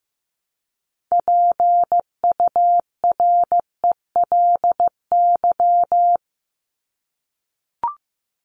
Morse code, heard as a keyed tone, about 15 wpm, 700 Hz. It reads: PURELY
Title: LY Suffix - Top 100 Words with Base Words 15wpm